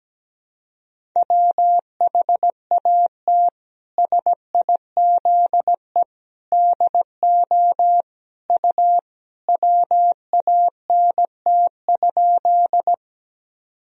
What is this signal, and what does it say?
Morse code, heard as a keyed tone, about 17 wpm, 700 Hz. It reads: WHAT SIZE DO U WANT?
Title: Sentences from Top 400 Words and Abbreviations 17wpm